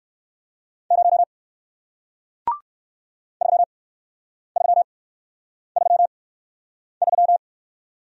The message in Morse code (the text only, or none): 0 3333